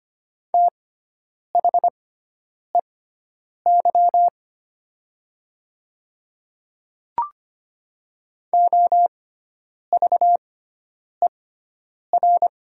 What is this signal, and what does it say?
Morse code, heard as a keyed tone, about 25 wpm, spaced as for 9 wpm, 700 Hz. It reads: THEY OVER
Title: 6x Character Spacing - Top 100 Words - 25wpm